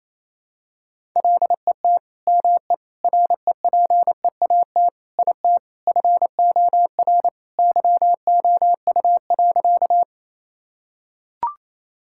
Morse code, heard as a keyed tone, about 28 wpm, 700 Hz. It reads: LET ME REPEAT IT FOR YOU.
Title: Sentences from Top 1600 Words 28wpm